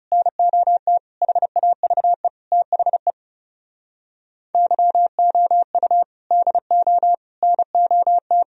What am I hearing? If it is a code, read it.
Morse code, heard as a keyed tone, about 38 wpm, 700 Hz. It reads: NOT HAVE THE YOU DO NOT